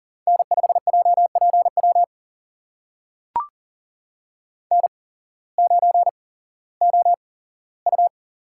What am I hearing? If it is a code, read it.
Morse code, heard as a keyed tone, about 40 wpm, spaced as for 19 wpm, 700 Hz. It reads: N5JPW N9OU
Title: USA Call Signs - 5 Characters - 8x Character Spacing - 40wpm